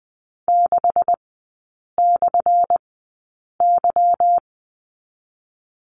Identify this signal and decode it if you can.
Morse code, heard as a keyed tone, about 20 wpm, spaced as for 8 wpm, 700 Hz. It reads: 6/Y
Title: 178 Lesson 29e - Slash - Three Characters 20wpm